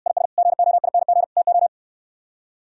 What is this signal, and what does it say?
Morse code, heard as a keyed tone, about 68 wpm, 700 Hz. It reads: IS GOING TO